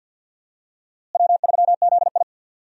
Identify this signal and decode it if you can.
Morse code, heard as a keyed tone, about 51 wpm, 700 Hz. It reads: W3ZI
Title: USA Call Signs - 4 Characters - Repeated 2x - 50wpm